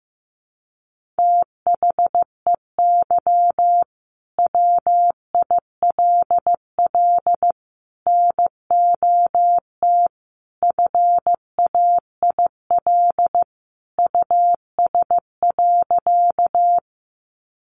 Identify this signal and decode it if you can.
Morse code, heard as a keyed tone, about 15 wpm, 700 Hz. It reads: THEY WILL NOT FAIL US.
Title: Sentences from Top 1100 Words 15wpm